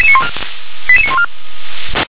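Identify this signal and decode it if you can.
Five-tone selective call (ZVEI), heard as quick runs of five tones, tones of about 70 ms. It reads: 88914
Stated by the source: ZVEI five-tone selective call received via SDR.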